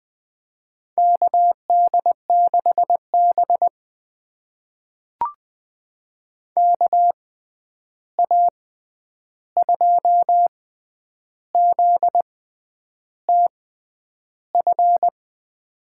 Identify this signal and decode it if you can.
Morse code, heard as a keyed tone, about 20 wpm, spaced as for 9 wpm, 700 Hz. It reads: KD6B KA2ZTF
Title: USA Call Signs - 6x Character Spacing - 20wpm